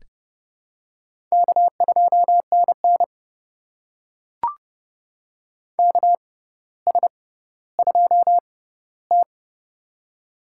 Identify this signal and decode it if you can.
Morse code, heard as a keyed tone, about 30 wpm, spaced as for 15 wpm, 700 Hz. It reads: K2NN KS2T
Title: USA Call Signs - 4 Characters - 6x Character Spacing - 30wpm